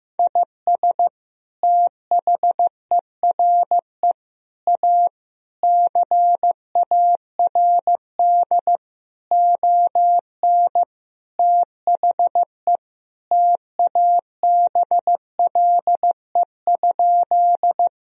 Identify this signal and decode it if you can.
Morse code, heard as a keyed tone, about 15 wpm, 700 Hz. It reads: IS THERE A CARD ON THE TABLE?
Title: Sentences from Top 1000 Words - Review Part 4 - 15wpm